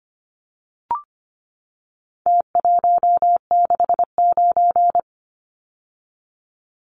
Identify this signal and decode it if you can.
Morse code, heard as a keyed tone, about 25 wpm, 700 Hz. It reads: T169
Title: Numbers - Contest Style - 4-Digit 25wpm